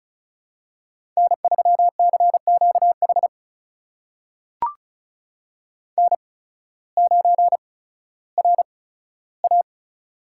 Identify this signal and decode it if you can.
Morse code, heard as a keyed tone, about 36 wpm, spaced as for 16 wpm, 700 Hz. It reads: N3CQH N9RA
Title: USA Call Signs - 8x Character Spacing - 35wpm